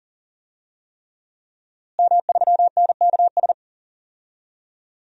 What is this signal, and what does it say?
Morse code, heard as a keyed tone, about 38 wpm, 700 Hz. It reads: M3NKS
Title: DXCC-100 Call Signs - 5 Characters - Repeated 3x - 40wpm